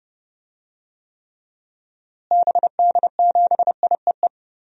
Morse code, heard as a keyed tone, about 30 wpm, 700 Hz. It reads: BD7IEE
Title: DXCC-100 Call Signs - 6 Characters - Repeated 2x - 30wpm